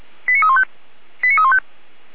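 Five-tone selective call (ZVEI), heard as quick runs of five tones, tones of about 70 ms. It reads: 89315 89315
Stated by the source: ZVEI five-tone selective call received on an SDR at 455.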